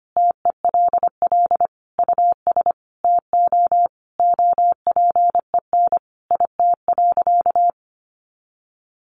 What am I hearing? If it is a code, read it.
Morse code, heard as a keyed tone, about 25 wpm, 700 Hz. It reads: TELL US TO OPEN IT.
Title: Sentences from Top 300 Words 25wpm